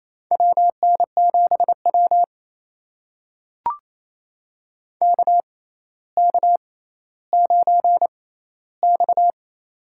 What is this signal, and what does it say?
Morse code, heard as a keyed tone, about 28 wpm, spaced as for 14 wpm, 700 Hz. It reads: WN7W KK9X
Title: USA Call Signs - 4 Characters - 6x Character Spacing - 28wpm